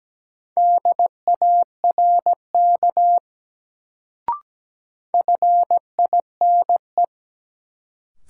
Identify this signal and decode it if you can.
Morse code, heard as a keyed tone, about 17 wpm, 700 Hz. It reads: DARK FINE
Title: Sentences from Top 500 Words - New Word Drill - 17wpm